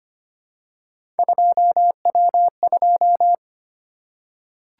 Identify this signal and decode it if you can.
Morse code, heard as a keyed tone, about 25 wpm, 700 Hz. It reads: 2W2
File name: Three Letter-Number Combinations - 25wpm